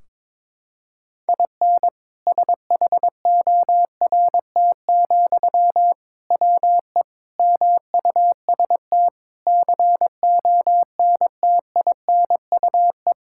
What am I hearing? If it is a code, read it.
Morse code, heard as a keyed tone, about 22 wpm, 700 Hz. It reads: IN SHORT, WE MUST CONTINUE
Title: Sentences from Top 1600 Words 22wpm